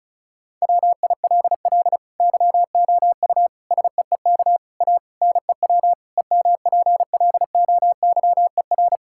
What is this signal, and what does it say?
Morse code, heard as a keyed tone, about 36 wpm, 700 Hz. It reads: WILL YOU SEEK A NEW EMPLOYER
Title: Sentences from Top 1600 Words 35wpm